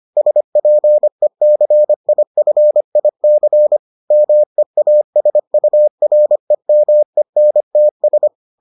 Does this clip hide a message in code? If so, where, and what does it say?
Yes: Morse code from 0.2 to 8.3 s, heard as a keyed tone, about 25 wpm, 600 Hz. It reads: SPECIFIC MEASUREMENTS